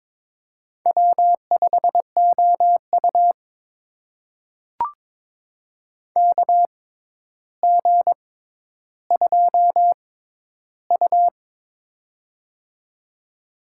Morse code, heard as a keyed tone, about 22 wpm, spaced as for 11 wpm, 700 Hz. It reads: W5OU KG2U
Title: USA Call Signs - 4 Characters - 6x Character Spacing - 22wpm